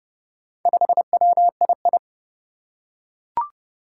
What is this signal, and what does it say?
Morse code, heard as a keyed tone, about 30 wpm, 700 Hz. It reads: 5WII
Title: Four Challenging Letter-Number Combinations 30wpm